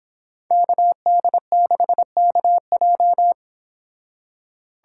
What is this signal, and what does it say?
Morse code, heard as a keyed tone, about 26 wpm, 700 Hz. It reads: KD6KJ